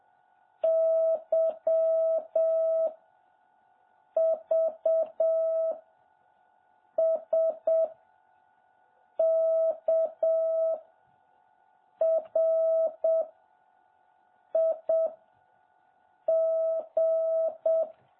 Morse code, heard as a keyed tone, about 7 wpm, 650 Hz. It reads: Y V S K R I G
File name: jh8cbh / 2021cw